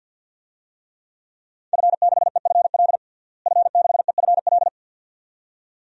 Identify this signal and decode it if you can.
Morse code, heard as a keyed tone, about 46 wpm, 700 Hz. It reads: F6EFL F6EFL
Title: DXCC-100 Call Signs - 5 Characters - Repeated 2x - 50wpm